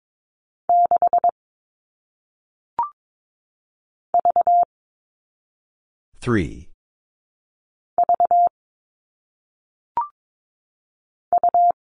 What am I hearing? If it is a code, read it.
Morse code, heard as a keyed tone, about 22 wpm, 700 Hz. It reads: 6 V V U
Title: All Cut Numbers - 1-Digit 22wpm